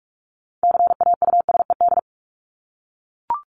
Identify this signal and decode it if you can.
Morse code, heard as a keyed tone, about 44 wpm, 700 Hz. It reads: CAUSED
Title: ED Suffix - Top 100 Words with Base Words 45wpm